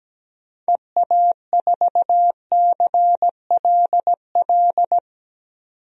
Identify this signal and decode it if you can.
Morse code, heard as a keyed tone, about 17 wpm, 700 Hz. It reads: EA4CLL
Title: DXCC-100 Call Signs - 6 Characters - 17wpm